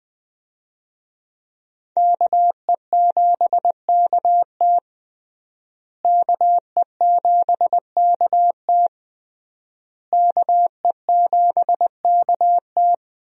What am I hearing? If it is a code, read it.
Morse code, heard as a keyed tone, about 20 wpm, 700 Hz. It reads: KE7KT KE7KT KE7KT
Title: USA Call Signs - 5 Characters - Repeated 3x - 20wpm